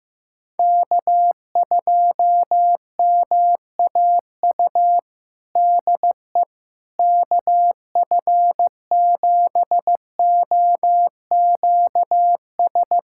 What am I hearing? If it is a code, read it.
Morse code, heard as a keyed tone, about 15 wpm, 700 Hz. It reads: K2MAU DE KF7OQS
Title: USA Callsign DE Callsign 15wpm